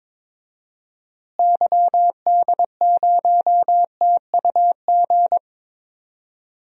Morse code, heard as a keyed tone, about 22 wpm, 700 Hz. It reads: YD0TUG
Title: DXCC-100 Call Signs - 6 Characters - Repeated 2x - 22wpm